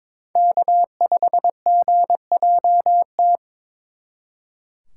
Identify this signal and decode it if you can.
Morse code, heard as a keyed tone, about 25 wpm, 700 Hz. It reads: K5GJT
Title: Speed Racing - USA Call Signs 22wpm